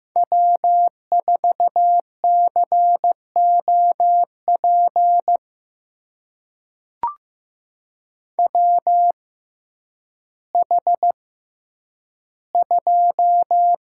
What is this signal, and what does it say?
Morse code, heard as a keyed tone, about 16 wpm, spaced as for 9 wpm, 700 Hz. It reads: W4COP WH2
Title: USA Call Signs - 6x Character Spacing - 15wpm